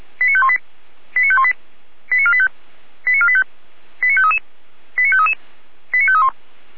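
Five-tone selective call (ZVEI), heard as quick runs of five tones, tones of about 70 ms. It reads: 89518 89518 89485 89485 89533 89533 89531
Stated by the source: Five-tone ZVEI selective call received on SDR.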